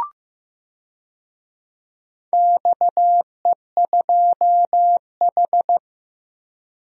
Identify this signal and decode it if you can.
Morse code, heard as a keyed tone, about 15 wpm, 700 Hz. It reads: XE2H
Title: DXCC-100 Call Signs - 4 Characters - 15wpm